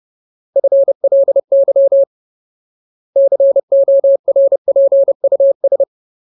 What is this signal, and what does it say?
Morse code, heard as a keyed tone, about 30 wpm, 550 Hz. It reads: FLY CORPUS
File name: Ultimate Word List / 30WPM Ultimate